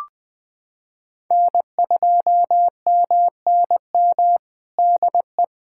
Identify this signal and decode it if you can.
Morse code, heard as a keyed tone, about 20 wpm, 700 Hz. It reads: N2MNM DE
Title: USA Callsign DE Callsign - 5 Characters - 20wpm